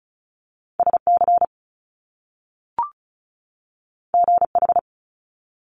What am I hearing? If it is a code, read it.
Morse code, heard as a keyed tone, about 36 wpm, 700 Hz. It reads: SC GH
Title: Top Two Letter Combinations in Words 35wpm